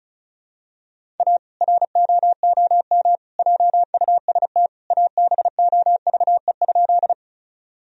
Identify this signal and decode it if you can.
Morse code, heard as a keyed tone, about 36 wpm, 700 Hz. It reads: A ROOM JUST ABOVE?
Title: Sentences from Top 1000 Words - Review Part 2 - 35wpm